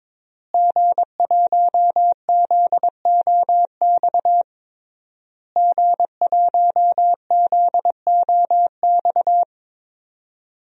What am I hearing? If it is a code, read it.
Morse code, heard as a keyed tone, about 22 wpm, 700 Hz. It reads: G1ZOX G1ZOX
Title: DXCC-100 Call Signs - 5 Characters - Repeated 3x - 22wpm